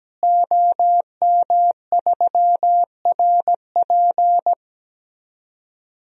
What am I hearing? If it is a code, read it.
Morse code, heard as a keyed tone, about 17 wpm, 700 Hz. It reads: OM3RP
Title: DXCC-100 Call Signs - 5 Characters - Repeated 3x - 17wpm